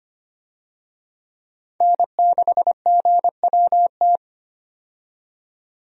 Morse code, heard as a keyed tone, about 25 wpm, 700 Hz. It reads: N6GWT
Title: USA Call Signs - 5 Characters - 25wpm